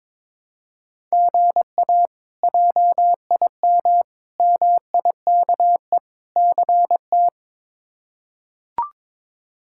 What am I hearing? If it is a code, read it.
Morse code, heard as a keyed tone, about 21 wpm, 700 Hz. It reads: GA JIM MIKE CT
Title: K1USN SST QSO Practice - 22wpm